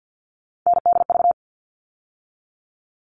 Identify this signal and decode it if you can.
Morse code, heard as a keyed tone, about 50 wpm, 700 Hz. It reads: NDV
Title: All Cut Numbers - 3-Digit 50wpm